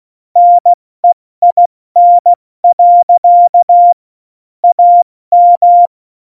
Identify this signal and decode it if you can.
Morse code, heard as a keyed tone, about 16 wpm, 700 Hz. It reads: NEIN. AM
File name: Morsekurs / Witze / Witz 14 April 2020, 70 F80